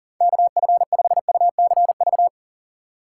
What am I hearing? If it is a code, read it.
Morse code, heard as a keyed tone, about 49 wpm, 700 Hz. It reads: KF5UCV
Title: Speed Racing - USA 2x3 Call Signs 40wpm